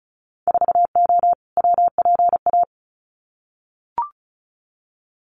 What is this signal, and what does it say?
Morse code, heard as a keyed tone, about 34 wpm, 700 Hz. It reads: 4O WPA
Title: Winter Field Day Exchanges 35wpm